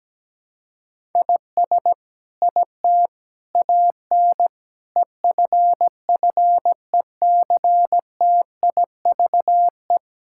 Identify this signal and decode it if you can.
Morse code, heard as a keyed tone, about 17 wpm, 700 Hz. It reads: IS IT AN EFFECTIVE
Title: Sentences from Top 1000 Words - Review Part 6 - 17wpm